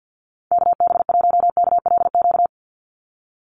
Morse code, heard as a keyed tone, about 49 wpm, 700 Hz. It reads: KB1KLQ
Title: USA Call Signs 50wpm